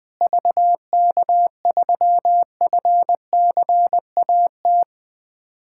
Morse code, heard as a keyed tone, about 20 wpm, 700 Hz. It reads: VK3FCAT
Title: USA Callsign DE Callsign 20wpm